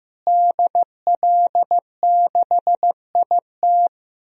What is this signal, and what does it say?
Morse code, heard as a keyed tone, about 15 wpm, 700 Hz. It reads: DL6IT
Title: DXCC-100 Call Signs - 5 Characters - Repeated 2x - 15wpm